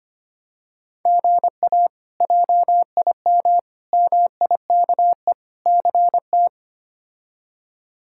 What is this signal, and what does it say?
Morse code, heard as a keyed tone, about 25 wpm, 700 Hz. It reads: GA JIM MIKE CT
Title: K1USN SST QSO Practice - 25wpm